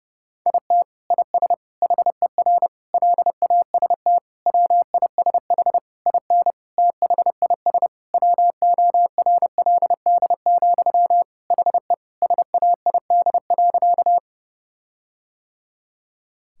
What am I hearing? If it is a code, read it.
Morse code, heard as a keyed tone, about 30 wpm, 700 Hz. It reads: IT IS HER LAST WISH IN THIS WORLD, HE SAID.
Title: Sentences from Top 1600 Words 30wpm